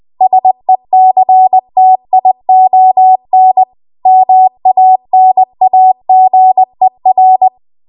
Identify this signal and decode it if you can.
Morse code, heard as a keyed tone, about 20 wpm, 750 Hz. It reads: SECTION MANAGER
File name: Morse / 20WPM